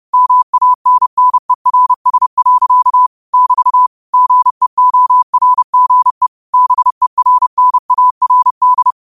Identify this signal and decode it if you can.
Morse code, heard as a keyed tone, about 30 wpm, 1000 Hz. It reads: MANNERS. <BT> GEORGE BERNARD
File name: qotdfiles / 30wpm